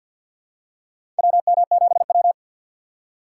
Morse code, heard as a keyed tone, about 50 wpm, 700 Hz. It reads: WM7W
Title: USA Call Signs - 3x Character Spacing - 50wpm